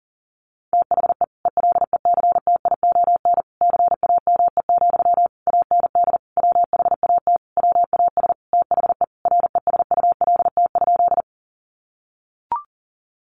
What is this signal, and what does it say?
Morse code, heard as a keyed tone, about 37 wpm, 700 Hz. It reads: THE ELECTION CAME, AND WHAT WAS THE RESULT?
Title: Sentences from Top 1500 Words 40wpm